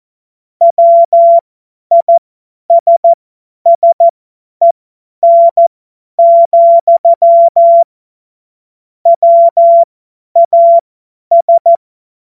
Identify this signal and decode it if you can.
Morse code, heard as a keyed tone, about 14 wpm, spaced as for 10 wpm, 680 Hz. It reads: WISSEN, WAS
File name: Morsekurs / Witze / Witz 25. Oktober 2022 50 F70